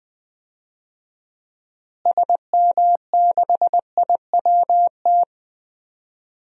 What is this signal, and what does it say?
Morse code, heard as a keyed tone, about 20 wpm, 700 Hz. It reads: SM6IWT
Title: DXCC-100 Call Signs - 6 Characters - Repeated 2x - 20wpm